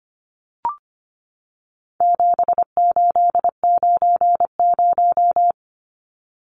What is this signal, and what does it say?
Morse code, heard as a keyed tone, about 25 wpm, 700 Hz. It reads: 7890
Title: Numbers - 4-Digit 25wpm